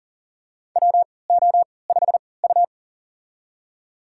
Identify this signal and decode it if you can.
Morse code, heard as a keyed tone, about 40 wpm, spaced as for 23 wpm, 700 Hz. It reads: WO5U
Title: USA Call Signs - 4 Characters - 3x Character Spacing - 40wpm